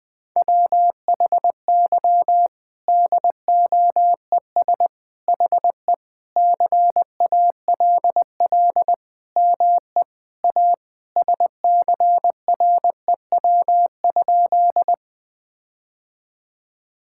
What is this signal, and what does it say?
Morse code, heard as a keyed tone, about 20 wpm, 700 Hz. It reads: WHY DOES HE CALL ME A SCREW?
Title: Sentences from Top 1600 Words 20wpm